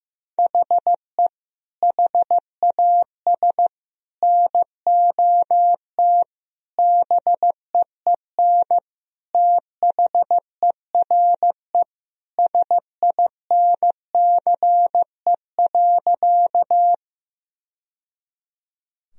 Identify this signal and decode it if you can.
Morse code, heard as a keyed tone, about 16 wpm, 700 Hz. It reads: HE HAS NOT BEEN THERE SINCE.
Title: Sentences from Top 1000 Words - Review Part 1 - 15wpm